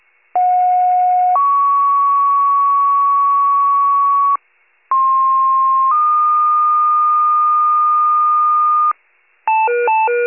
Two-tone page, heard as a one-second tone, then a three-second tone, about 730 then 1120 Hz, about 1060 then 1220 Hz. It is heard from 0.4 to 8.9 s.
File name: wcalert